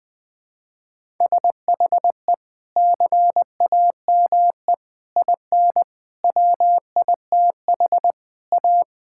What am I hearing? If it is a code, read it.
Morse code, heard as a keyed tone, about 21 wpm, 700 Hz. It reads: SHE CAME IN WITH A
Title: Sentences with 2-Word Phrases - 20wpm